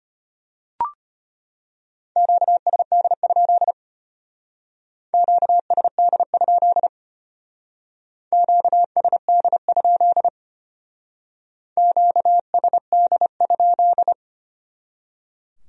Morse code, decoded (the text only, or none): QSD? QSD? QSD? QSD?